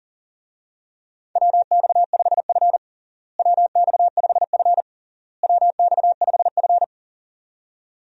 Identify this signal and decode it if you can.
Morse code, heard as a keyed tone, about 40 wpm, 700 Hz. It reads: WX5F WX5F WX5F
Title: USA Call Signs - 4 Characters - Repeated 3x - 40wpm